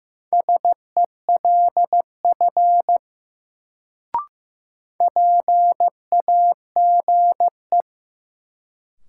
Morse code, heard as a keyed tone, about 15 wpm, 700 Hz. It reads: SELF PAGE
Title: Sentences from Top 1000 Words - Review Part 1 - New Word Drill - 15wpm